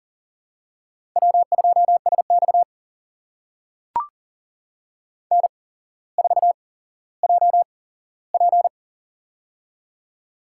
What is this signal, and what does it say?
Morse code, heard as a keyed tone, about 40 wpm, spaced as for 17 wpm, 700 Hz. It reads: W2SX N4JP
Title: USA Call Signs - 4 Characters - 8x Character Spacing - 40wpm